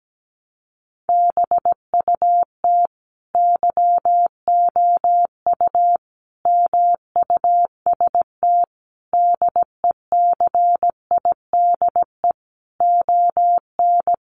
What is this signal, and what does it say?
Morse code, heard as a keyed tone, about 17 wpm, 700 Hz. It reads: BUT YOU MUST DECIDE ON
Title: Sentences from Top 600 Words 17wpm